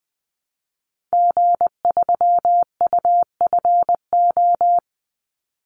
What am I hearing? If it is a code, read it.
Morse code, heard as a keyed tone, about 20 wpm, 700 Hz. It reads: G3UFO
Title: Call Signs - 20wpm